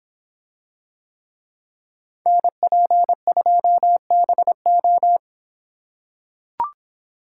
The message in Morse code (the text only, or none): NP2BO